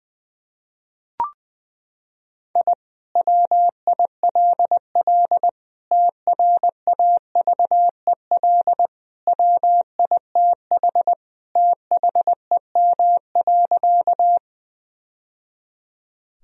Morse code, heard as a keyed tone, about 21 wpm, 700 Hz. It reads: I WILL TRAVEL WITH THEM.
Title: Sentences from Top 1000 Words - Review Part 2 - 20wpm